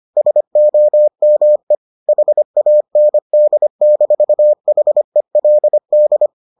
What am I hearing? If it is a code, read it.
Morse code, heard as a keyed tone, about 25 wpm, 600 Hz. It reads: SOME HAND-HELD